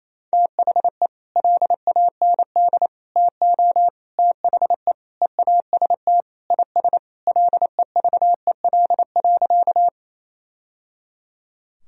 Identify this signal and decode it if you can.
Morse code, heard as a keyed tone, about 28 wpm, 700 Hz. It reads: THE LAND TO THE EAST IS LEVEL.
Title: Sentences from Top 1000 Words - Review Part 4 - 28wpm